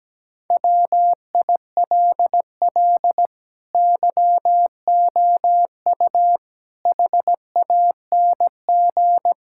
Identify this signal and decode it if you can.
Morse code, heard as a keyed tone, about 17 wpm, 700 Hz. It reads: WILL YOU HANG
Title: Sentences from Top 1600 Words 17wpm